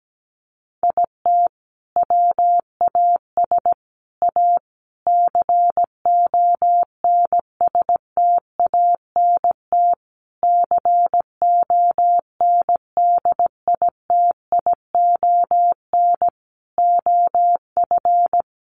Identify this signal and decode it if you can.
Morse code, heard as a keyed tone, about 17 wpm, 700 Hz. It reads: IT WAS A CONSTANT CONDITION OF
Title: Sentences from Top 1500 Words 17wpm